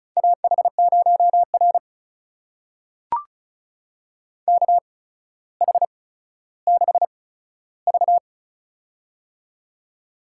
Morse code, heard as a keyed tone, about 36 wpm, spaced as for 15 wpm, 700 Hz. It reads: AH0R KH6V
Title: USA Call Signs - 8x Character Spacing - 35wpm